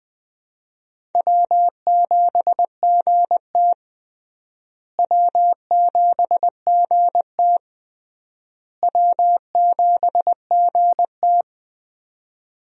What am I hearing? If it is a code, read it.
Morse code, heard as a keyed tone, about 20 wpm, 700 Hz. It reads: W7GT W7GT W7GT
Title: USA Call Signs - 4 Characters - Repeated 3x - 20wpm